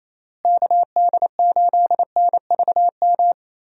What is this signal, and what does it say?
Morse code, heard as a keyed tone, about 28 wpm, 700 Hz. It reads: KD8NVM
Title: USA Call Signs - 6 Characters - Repeated 2x - 28wpm